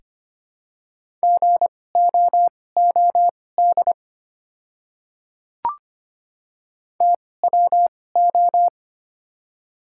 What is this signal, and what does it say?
Morse code, heard as a keyed tone, about 25 wpm, spaced as for 18 wpm, 700 Hz. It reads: GOOD TWO
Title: 2x Character Spacing - Top 500 Words - 25wpm